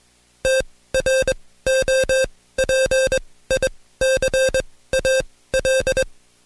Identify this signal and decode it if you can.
Morse code, heard as a keyed tone, about 21 wpm, spaced as for 15 wpm, 530 Hz. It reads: TROPICAL
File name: Code Quick / AudioDownld2 / Seg134-Stepper7 15WPM